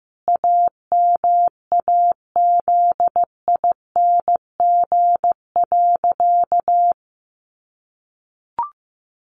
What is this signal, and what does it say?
Morse code, heard as a keyed tone, about 15 wpm, 700 Hz. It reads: AMAZING.
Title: Sentences from Top 1500 Words 15wpm